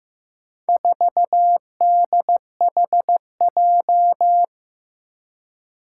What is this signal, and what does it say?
Morse code, heard as a keyed tone, about 16 wpm, 700 Hz. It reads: 4DHJ
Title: Four Challenging Letter-Number Combinations 15wpm